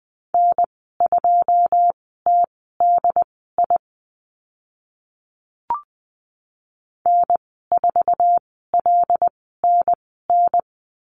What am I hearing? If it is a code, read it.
Morse code, heard as a keyed tone, about 20 wpm, spaced as for 14 wpm, 700 Hz. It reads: N2TDI N4LNN
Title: 082 Lesson 14g - Number 2 - Call Signs 20wpm